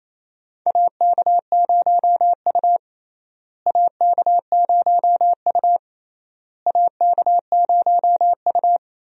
Morse code, heard as a keyed tone, about 27 wpm, 700 Hz. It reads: AK0U AK0U AK0U
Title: USA Call Signs - 4 Characters - Repeated 3x - 28wpm